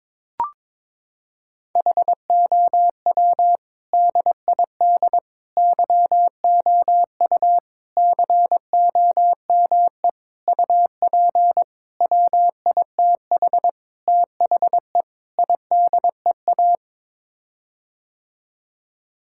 Morse code, heard as a keyed tone, about 22 wpm, 700 Hz. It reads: HOW DID YOU COME UP WITH THE IDEA
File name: practice / 22wpm / Sets of 8 Words - 22wpm